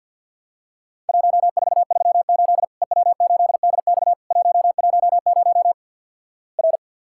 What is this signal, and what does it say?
Morse code, heard as a keyed tone, about 50 wpm, 700 Hz. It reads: 1438 EW8DX 110 R
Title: Worked All Europe DX Contest - QTC Exchange - 50wpm